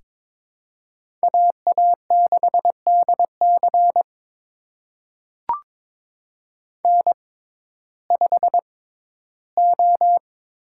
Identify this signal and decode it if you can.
Morse code, heard as a keyed tone, about 22 wpm, spaced as for 14 wpm, 700 Hz. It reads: AA6DC N5O
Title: USA Call Signs - 6x Character Spacing - 22wpm